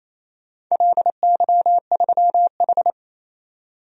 Morse code, heard as a keyed tone, about 28 wpm, 700 Hz. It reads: LY3H
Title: DXCC-100 Call Signs - 4 Characters - 28wpm